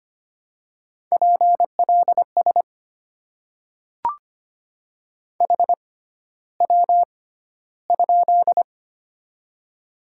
Morse code, heard as a keyed tone, about 25 wpm, spaced as for 13 wpm, 700 Hz. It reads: PLS HW?
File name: Top 90 QSO Elements - 6x Character Spacing - 25wpm